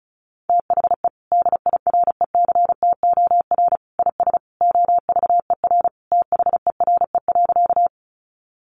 Morse code, heard as a keyed tone, about 34 wpm, 700 Hz. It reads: THE DIRECTOR IS OVER THERE.
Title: Sentences from Top 1300 Words 35wpm